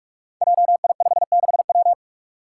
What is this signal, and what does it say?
Morse code, heard as a keyed tone, about 41 wpm, 700 Hz. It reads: JI56W
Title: Five Challenging Letter-Number Combinations 45wpm